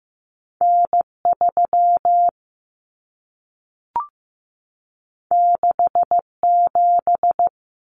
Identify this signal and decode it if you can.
Morse code, heard as a keyed tone, about 15 wpm, 700 Hz. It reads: N3 67
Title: Common Cut Numbers - 2-Digit 15wpm